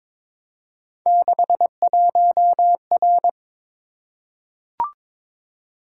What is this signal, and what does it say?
Morse code, heard as a keyed tone, about 22 wpm, 700 Hz. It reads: 61R
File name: practice / 22wpm / Three Letter-Number Combinations - 22wpm